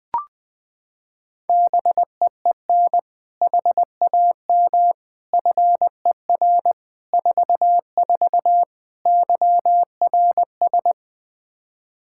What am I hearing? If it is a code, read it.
Morse code, heard as a keyed tone, about 21 wpm, 700 Hz. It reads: BEEN HAM FER 44 YRS
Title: Practice QSO - Exchange 3 - More Info - 20wpm